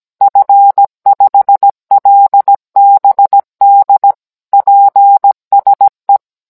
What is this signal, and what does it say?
Morse code, heard as a keyed tone, about 17 wpm, 800 Hz. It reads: F5LBD PSE